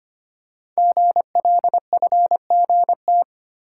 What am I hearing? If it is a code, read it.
Morse code, heard as a keyed tone, about 25 wpm, 700 Hz. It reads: GLFGT
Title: Code Groups - 5 Letters - 25wpm